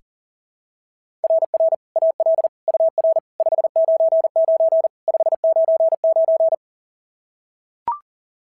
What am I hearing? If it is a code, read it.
Morse code, heard as a keyed tone, about 40 wpm, 690 Hz. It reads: RR AL UR 599 599
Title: SOTA QSO Practice 40wpm